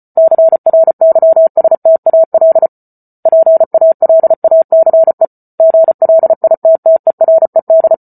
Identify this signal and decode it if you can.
Morse code, heard as a keyed tone, about 34 wpm, 650 Hz. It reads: CRYSTAL PALACE GLITTERED